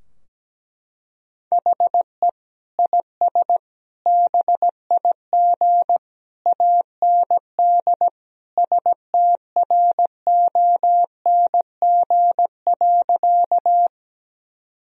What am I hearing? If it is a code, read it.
Morse code, heard as a keyed tone, about 17 wpm, 700 Hz. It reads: HE IS BIG AND STRONG.
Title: Sentences from Top 1000 Words - Review Part 3 - 17wpm